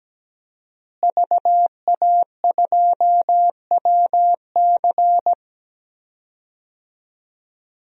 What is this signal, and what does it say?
Morse code, heard as a keyed tone, about 17 wpm, 700 Hz. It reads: VA2WC
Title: DXCC-100 Call Signs - 5 Characters - Repeated 3x - 17wpm